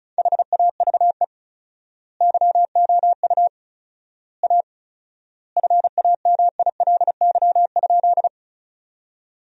Morse code, heard as a keyed tone, about 36 wpm, 700 Hz. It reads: HAVE YOU A FAMILY?
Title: 4x Word-Spacing - Sentences from Top 400 Words - 35wpm